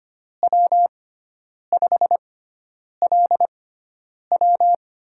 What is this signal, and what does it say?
Morse code, heard as a keyed tone, about 25 wpm, spaced as for 9 wpm, 700 Hz. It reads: W5LW
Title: USA Call Signs - 6x Character Spacing - 25wpm